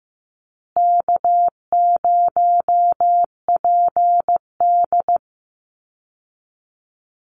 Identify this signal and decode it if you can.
Morse code, heard as a keyed tone, about 15 wpm, 700 Hz. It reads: K0PD
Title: Easy Call Signs Repeated 3x - 15wpm